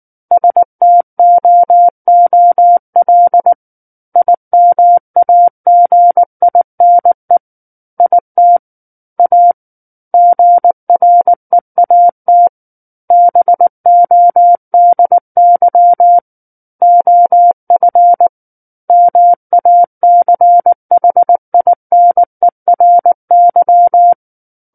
Morse code, heard as a keyed tone, about 19 wpm, 700 Hz. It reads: STOOL IMAGINE IT A GREAT BODY OF MACHINERY
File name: Book One / 10-Chapter 10 - 19 WPM